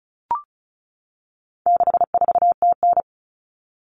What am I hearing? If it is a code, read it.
Morse code, heard as a keyed tone, about 34 wpm, 700 Hz. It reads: 64TN
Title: Common Cut Numbers - 4-Digit 35wpm